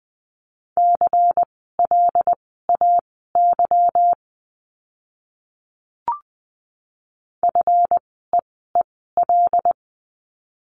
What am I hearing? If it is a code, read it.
Morse code, heard as a keyed tone, about 20 wpm, spaced as for 15 wpm, 700 Hz. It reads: CLAY FEEL
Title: 146 Lesson 24f - Letter y - Words 20wpm